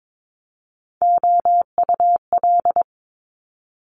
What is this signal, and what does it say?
Morse code, heard as a keyed tone, about 22 wpm, 700 Hz. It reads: OUL
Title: Top Three Letter Combinations in Words 22wpm